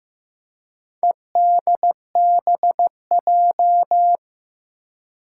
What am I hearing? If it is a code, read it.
Morse code, heard as a keyed tone, about 15 wpm, 700 Hz. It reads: EDBJ
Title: Four Challenging Letter-Number Combinations 15wpm